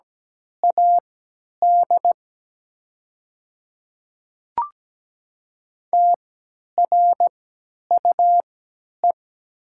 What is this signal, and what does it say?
Morse code, heard as a keyed tone, about 17 wpm, spaced as for 10 wpm, 700 Hz. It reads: AD TRUE
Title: 3x Character Spacing - Top 500 Words - 17wpm